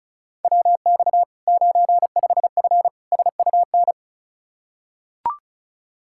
Morse code, heard as a keyed tone, about 36 wpm, 700 Hz. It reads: WX 95F SUN
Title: Practice QSO - Exchange 2 - Info - 35wpm